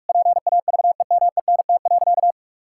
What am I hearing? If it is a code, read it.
Morse code, heard as a keyed tone, about 48 wpm, 700 Hz. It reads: PAVEMENT.
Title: Pangram Challenge v2 - 45wpm